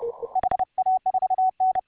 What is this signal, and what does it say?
Morse code, heard as a keyed tone, about 30 wpm, 750 Hz. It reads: HA4N